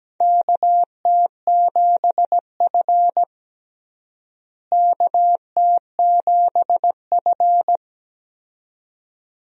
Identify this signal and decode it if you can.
Morse code, heard as a keyed tone, about 17 wpm, 700 Hz. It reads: KT7F KT7F
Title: USA Call Signs - 4 Characters - Repeated 2x - 17wpm